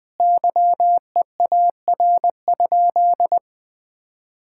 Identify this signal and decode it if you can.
Morse code, heard as a keyed tone, about 20 wpm, 700 Hz. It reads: YEAR?
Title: Sentences from Top 1000 Words - Review Part 5 - 20wpm